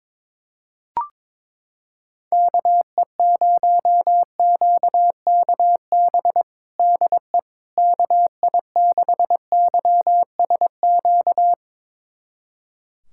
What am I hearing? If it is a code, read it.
Morse code, heard as a keyed tone, about 22 wpm, 700 Hz. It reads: KE0QKB DE KI6YSQ
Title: USA Callsign DE Callsign - 6 Characters - 22wpm